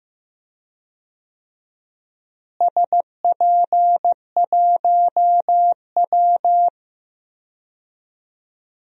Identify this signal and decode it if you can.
Morse code, heard as a keyed tone, about 15 wpm, 700 Hz. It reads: SP1W
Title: DXCC-100 Call Signs - 4 Characters - Repeated 3x - 15wpm